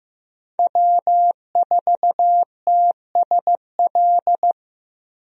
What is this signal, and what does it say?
Morse code, heard as a keyed tone, about 16 wpm, 700 Hz. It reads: W4TSL
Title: USA Call Signs - 5 Characters - Repeated 2x - 15wpm